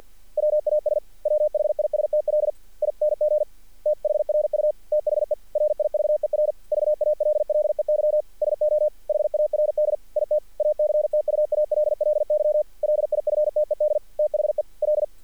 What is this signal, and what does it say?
Morse code, heard as a keyed tone, about 42 wpm, 590 Hz. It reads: WAS WHISTLING THRU THE RIVER VALLEY SO HARD IT ACTUALLY LIFTED THE L
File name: archived / CRSnet